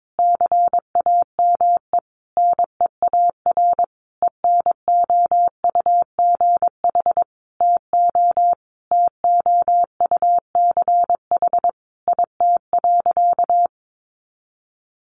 Morse code, heard as a keyed tone, about 22 wpm, 700 Hz. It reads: CAME NEAR ENOUGH TO TOUCH IT.
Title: Sentences from Top 1200 Words 22wpm